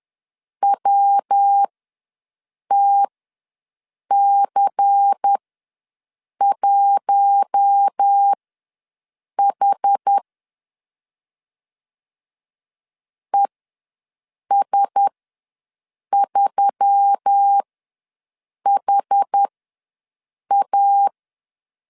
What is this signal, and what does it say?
Morse code, heard as a keyed tone, about 10 wpm, spaced as for 6 wpm, 800 Hz. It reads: WTC1H ES3HA
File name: LEZIONI / 5 LBVC / 5 LBVC ws 10wpm 3733